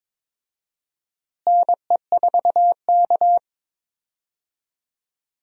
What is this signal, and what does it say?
Morse code, heard as a keyed tone, about 21 wpm, 700 Hz. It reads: NE4K
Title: USA Call Signs - 4 Characters - Repeated 3x - 22wpm